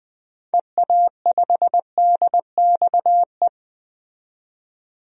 Morse code, heard as a keyed tone, about 20 wpm, 700 Hz. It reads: EA5DXE
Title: DXCC-100 Call Signs - 6 Characters - 20wpm